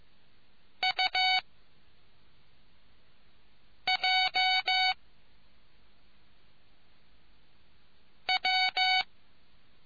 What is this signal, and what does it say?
Morse code, heard as a keyed tone, about 15 wpm, 750 Hz. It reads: U J W